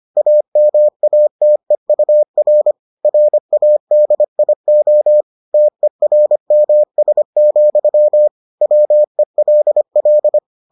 Morse code, heard as a keyed tone, about 25 wpm, 600 Hz. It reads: AMATEUR RADIO TERMS, WELL